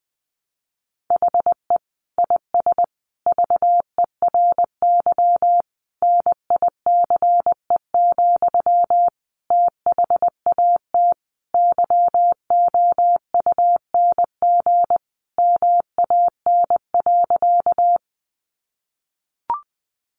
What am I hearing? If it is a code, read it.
Morse code, heard as a keyed tone, about 20 wpm, 700 Hz. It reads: HE IS VERY NICE, THAT YOUNG MAN.
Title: Sentences from Top 1100 Words 20wpm